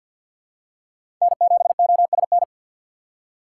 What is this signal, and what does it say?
Morse code, heard as a keyed tone, about 53 wpm, 700 Hz. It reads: N7OSN
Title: USA Call Signs - 5 Characters - 2x Character Spacing - 50wpm